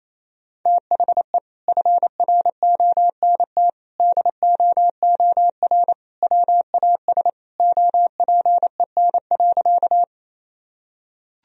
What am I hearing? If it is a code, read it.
Morse code, heard as a keyed tone, about 28 wpm, 700 Hz. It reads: THE FRONT DOOR WAS OPEN.
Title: Sentences from Top 1000 Words - Review Part 2 - 28wpm